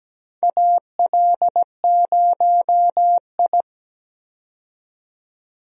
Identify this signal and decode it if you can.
Morse code, heard as a keyed tone, about 17 wpm, 700 Hz. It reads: AL0I
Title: USA Call Signs - 4 Characters - 17wpm